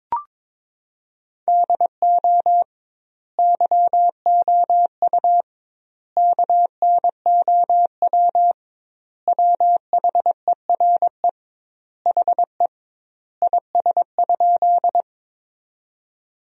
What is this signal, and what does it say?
Morse code, heard as a keyed tone, about 22 wpm, 700 Hz. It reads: DO YOU KNOW WHERE HE IS?
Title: 2x Word-Spacing - Sentences from Top 300 Words - 22wpm